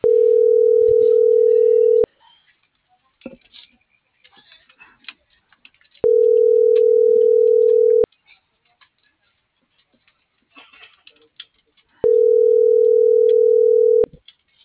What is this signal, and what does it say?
Ringback tone (North America), 440 + 480 Hz.